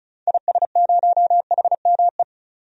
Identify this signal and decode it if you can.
Morse code, heard as a keyed tone, about 36 wpm, 700 Hz. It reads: IS0HME
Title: DXCC-100 Call Signs - 6 Characters - 35wpm